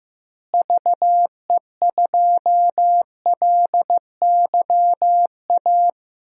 Morse code, heard as a keyed tone, about 15 wpm, 700 Hz. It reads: VE2LYA
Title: DXCC-100 Call Signs - 6 Characters - Repeated 2x - 15wpm